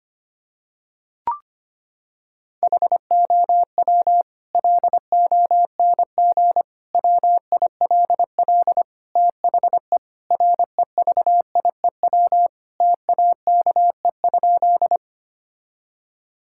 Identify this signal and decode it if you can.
Morse code, heard as a keyed tone, about 25 wpm, 700 Hz. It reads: HOW LONG WILL THE REVIEW TAKE?
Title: Sentences from Top 1000 Words - Review Part 6 - 25wpm